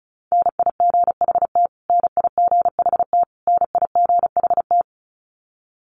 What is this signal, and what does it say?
Morse code, heard as a keyed tone, about 34 wpm, 700 Hz. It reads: NIGHT NIGHT NIGHT
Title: Top 300 Words Repeated 35wpm